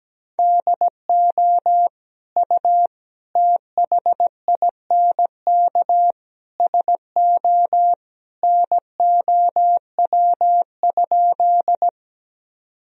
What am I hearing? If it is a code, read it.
Morse code, heard as a keyed tone, about 17 wpm, 700 Hz. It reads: DO U THINK SO NOW?
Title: Sentences from Top 200 Words and Abbreviations 17wpm